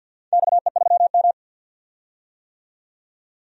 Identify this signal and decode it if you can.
Morse code, heard as a keyed tone, about 46 wpm, 700 Hz. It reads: XE3M
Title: DXCC-100 Call Signs - 4 Characters - 50wpm